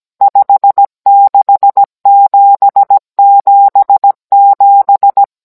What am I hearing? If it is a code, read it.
Morse code, heard as a keyed tone, about 17 wpm, 800 Hz. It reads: 56777